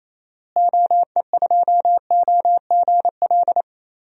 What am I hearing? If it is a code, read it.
Morse code, heard as a keyed tone, about 27 wpm, 700 Hz. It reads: OE2OGL
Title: DXCC-100 Call Signs - 6 Characters - Repeated 2x - 28wpm